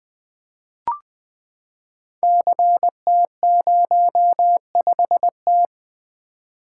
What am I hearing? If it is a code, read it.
Morse code, heard as a keyed tone, about 20 wpm, 700 Hz. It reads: CT05T
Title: Code Groups - 5 Letter:Numbers - 20wpm